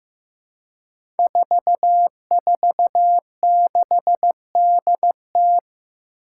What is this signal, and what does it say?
Morse code, heard as a keyed tone, about 15 wpm, 700 Hz. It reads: 446DT
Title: Five Challenging Letter-Number Combinations 15wpm